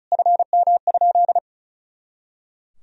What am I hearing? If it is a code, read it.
Morse code, heard as a keyed tone, about 34 wpm, 700 Hz. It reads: FM?
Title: Sentences from Top 400 Words and Abbreviations 35wpm